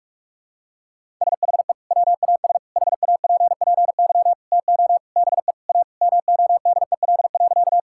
Morse code, heard as a keyed tone, about 45 wpm, 700 Hz. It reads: SHE WAS HAPPY TO BE A MODEL.
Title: Sentences from Top 1000 Words - Review Part 5 - 45wpm